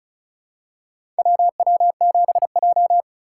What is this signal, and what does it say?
Morse code, heard as a keyed tone, about 34 wpm, 700 Hz. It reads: WW7J
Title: USA Call Signs - 4 Characters - 3x Character Spacing - 35wpm